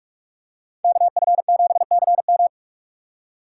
Morse code, heard as a keyed tone, about 44 wpm, 700 Hz. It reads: KF7CM
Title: USA Call Signs - 5 Characters - 6x Character Spacing - 45wpm